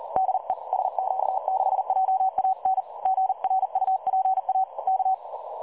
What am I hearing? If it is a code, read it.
Morse code, heard as a keyed tone, about 36 wpm, 760 Hz. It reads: DE SQ6JAN GRUPA K
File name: dlazaawansowanych / nagraniazpasma / websdr recording 3565.0kHz